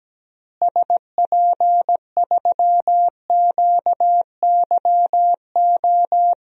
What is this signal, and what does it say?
Morse code, heard as a keyed tone, about 17 wpm, 700 Hz. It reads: SP3QYO
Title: DXCC-100 Call Signs - 6 Characters - Repeated 3x - 17wpm